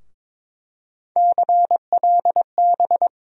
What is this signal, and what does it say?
Morse code, heard as a keyed tone, about 23 wpm, 700 Hz. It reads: CLB
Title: USA Call Sign Postfixes - 3 Letters 22wpm